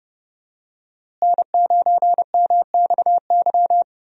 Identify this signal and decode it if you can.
Morse code, heard as a keyed tone, about 29 wpm, 700 Hz. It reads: N9MXY
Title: USA Call Signs - 5 Characters - Repeated 3x - 30wpm